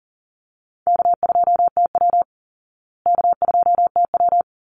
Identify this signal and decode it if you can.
Morse code, heard as a keyed tone, about 40 wpm, 700 Hz. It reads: K2TW K2TW
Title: Easy Call Signs Repeated 2x - 40wpm